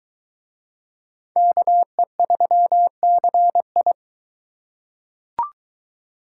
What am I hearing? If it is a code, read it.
Morse code, heard as a keyed tone, about 22 wpm, 700 Hz. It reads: KE3CI